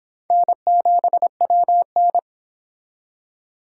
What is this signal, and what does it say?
Morse code, heard as a keyed tone, about 24 wpm, 700 Hz. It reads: N7WN